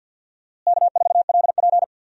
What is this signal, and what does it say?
Morse code, heard as a keyed tone, about 52 wpm, 700 Hz. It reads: K4LP
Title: USA Call Signs - 4 Characters - Repeated 2x - 50wpm